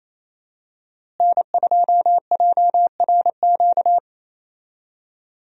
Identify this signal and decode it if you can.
Morse code, heard as a keyed tone, about 28 wpm, 700 Hz. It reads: N2JRQ
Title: USA Call Signs - 5 Characters - Repeated 3x - 28wpm